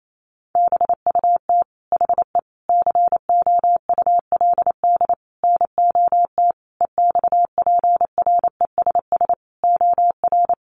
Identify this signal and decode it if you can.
Morse code, heard as a keyed tone, about 28 wpm, 700 Hz. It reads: BUT HE COULD NOT EXPRESS OR
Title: Sentences from Top 1100 Words 28wpm